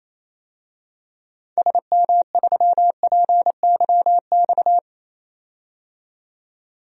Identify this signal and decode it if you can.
Morse code, heard as a keyed tone, about 28 wpm, 700 Hz. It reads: SM3PYX
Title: DXCC-100 Call Signs - 6 Characters - 28wpm